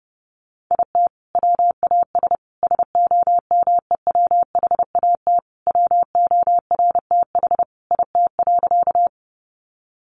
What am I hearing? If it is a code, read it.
Morse code, heard as a keyed tone, about 30 wpm, 700 Hz. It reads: IT WAS SOMEWHAT WORTH IT.
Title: Sentences from Top 1300 Words 30wpm